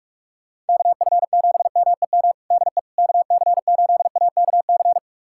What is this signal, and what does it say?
Morse code, heard as a keyed tone, about 44 wpm, 700 Hz. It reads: KF7MEM DE KC8AKC
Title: USA Callsign DE Callsign - 6 Characters - 45wpm